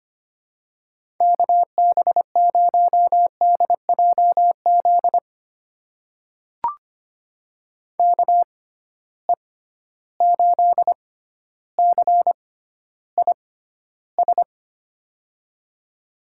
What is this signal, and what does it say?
Morse code, heard as a keyed tone, about 25 wpm, spaced as for 13 wpm, 700 Hz. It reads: KB0DJZ KE8CIS
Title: USA Call Signs - 6 Characters - 6x Character Spacing - 25wpm